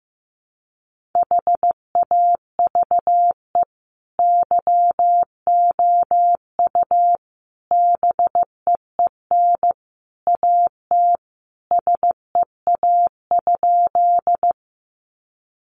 Morse code, heard as a keyed tone, about 14 wpm, 700 Hz. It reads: HAVE YOU BEEN AT SEA?
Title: Sentences from Top 300 Words 15wpm